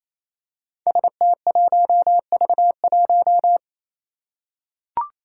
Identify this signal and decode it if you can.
Morse code, heard as a keyed tone, about 28 wpm, 700 Hz. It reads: ST1V1
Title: Five Challenging Letter-Number Combinations 28wpm